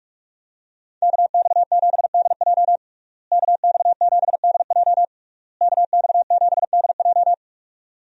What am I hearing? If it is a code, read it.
Morse code, heard as a keyed tone, about 46 wpm, 700 Hz. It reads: KX7DJ KX7DJ KX7DJ
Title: USA Call Signs - 5 Characters - Repeated 3x - 45wpm